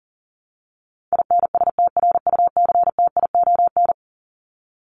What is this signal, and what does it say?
Morse code, heard as a keyed tone, about 40 wpm, 700 Hz. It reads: INSTRUCTION
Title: Top 1751-2000 Words 40wpm